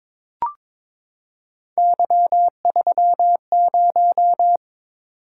Morse code, heard as a keyed tone, about 22 wpm, 700 Hz. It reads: Y30
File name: practice / 22wpm / Three Letter-Number Combinations - 22wpm